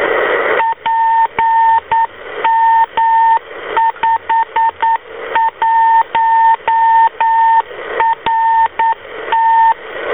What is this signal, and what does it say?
Morse code, heard as a keyed tone, about 9 wpm, 910 Hz, with radio noise. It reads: PM51RT